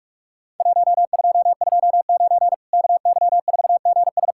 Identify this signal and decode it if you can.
Morse code, heard as a keyed tone, about 46 wpm, 700 Hz. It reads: 1229 KY4GS
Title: Worked All Europe DX Contest - QTC Exchange - 45wpm